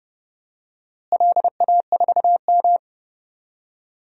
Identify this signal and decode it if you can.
Morse code, heard as a keyed tone, about 32 wpm, 700 Hz. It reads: LA4M
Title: DXCC-100 Call Signs - 4 Characters - Repeated 3x - 30wpm